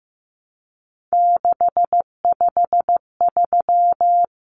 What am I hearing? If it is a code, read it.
Morse code, heard as a keyed tone, about 15 wpm, 700 Hz. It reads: 653
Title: Numbers - 3-Digit 15wpm